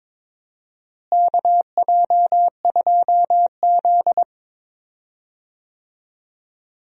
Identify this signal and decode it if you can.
Morse code, heard as a keyed tone, about 22 wpm, 700 Hz. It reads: KJ2Z
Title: USA Call Signs - 4 Characters - Repeated 2x - 22wpm